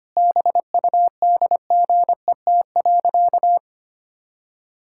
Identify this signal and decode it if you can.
Morse code, heard as a keyed tone, about 25 wpm, 700 Hz. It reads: BUDGET.
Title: Pangram Challenge v2 - 25wpm